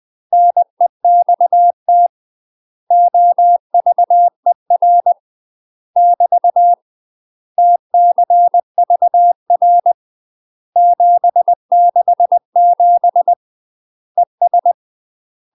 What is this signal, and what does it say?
Morse code, heard as a keyed tone, about 20 wpm, 700 Hz. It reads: NEXT OVER <BT> TCVR 767 ES
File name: UK MASTER QSOs TRAINING / 20WPM / UK MASTER QSOs 0027